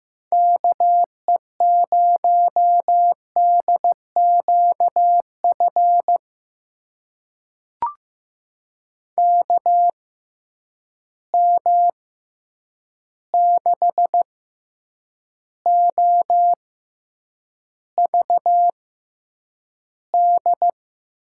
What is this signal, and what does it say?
Morse code, heard as a keyed tone, about 15 wpm, spaced as for 8 wpm, 700 Hz. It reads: KE0DQF KM6OVD
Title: USA Call Signs - 6 Characters - 6x Character Spacing - 15wpm